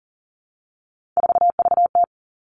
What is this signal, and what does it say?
Morse code, heard as a keyed tone, about 40 wpm, 700 Hz. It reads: 4VT
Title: All Cut Numbers - 3-Digit 40wpm